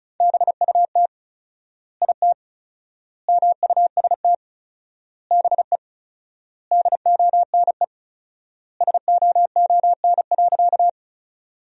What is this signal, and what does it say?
Morse code, heard as a keyed tone, about 36 wpm, 700 Hz. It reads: BUT IT MUST BE DONE SOON.
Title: 4x Word-Spacing - Sentences from Top 500 Words - 35wpm